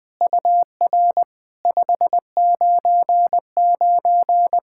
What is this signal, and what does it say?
Morse code, heard as a keyed tone, about 20 wpm, 700 Hz. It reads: UR 599
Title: SOTA QSO Practice 20wpm